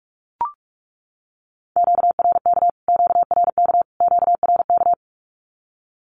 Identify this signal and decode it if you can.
Morse code, heard as a keyed tone, about 44 wpm, 700 Hz. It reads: QRK QRK QRK
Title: Q-Codes Repeated 3x 45wpm